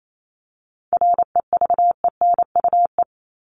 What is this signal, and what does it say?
Morse code, heard as a keyed tone, about 28 wpm, 700 Hz. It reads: REVENUE
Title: Top 1501-1750 Words 28wpm